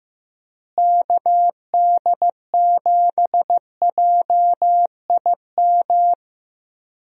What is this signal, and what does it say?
Morse code, heard as a keyed tone, about 15 wpm, 700 Hz. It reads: KD7JIM
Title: USA Call Signs - 6 Characters - Repeated 3x - 15wpm